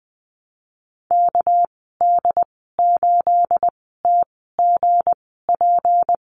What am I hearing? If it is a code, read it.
Morse code, heard as a keyed tone, about 20 wpm, spaced as for 14 wpm, 700 Hz. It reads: KD8TGP
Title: 231 Lesson 37g - Letter x - Call Signs 20wpm